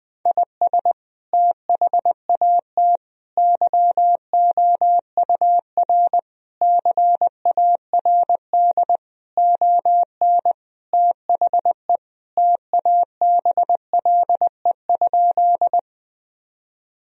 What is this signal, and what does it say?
Morse code, heard as a keyed tone, about 21 wpm, 700 Hz. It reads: IS THAT YOUR CARD ON THE TABLE?
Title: Sentences from Top 1000 Words - Review Part 4 - 20wpm